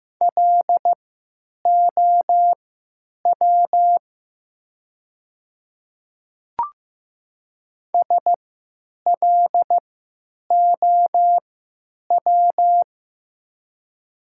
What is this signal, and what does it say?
Morse code, heard as a keyed tone, about 16 wpm, spaced as for 9 wpm, 700 Hz. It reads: LOW SLOW
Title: 3x Character Spacing - Top 500 Words - 15wpm